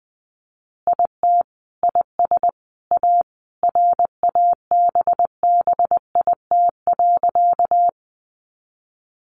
Morse code, heard as a keyed tone, about 20 wpm, 700 Hz. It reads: IT IS A RABBIT.
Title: Easy Phrases 20wpm